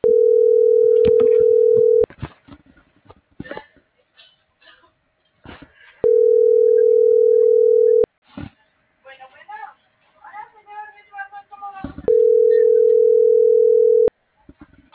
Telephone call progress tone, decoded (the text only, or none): ringback tone